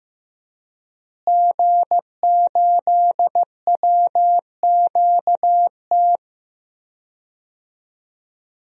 Morse code, heard as a keyed tone, about 16 wpm, 700 Hz. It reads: G8WQT
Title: DXCC-100 Call Signs - 5 Characters - Repeated 2x - 15wpm